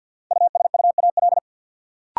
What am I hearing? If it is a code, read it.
Morse code, heard as a keyed tone, about 50 wpm, 700 Hz. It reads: USUAL
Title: Sentences from Top 400 Words - New Word Drill - 50wpm